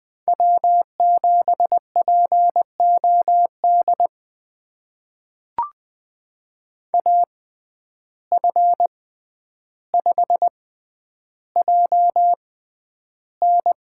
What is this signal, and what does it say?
Morse code, heard as a keyed tone, about 20 wpm, spaced as for 10 wpm, 700 Hz. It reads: W7POD AF5JN
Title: USA Call Signs - 5 Characters - 6x Character Spacing - 20wpm